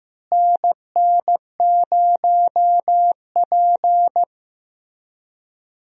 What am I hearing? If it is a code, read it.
Morse code, heard as a keyed tone, about 15 wpm, 700 Hz. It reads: NN0P
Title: USA Call Signs - 4 Characters - Repeated 3x - 15wpm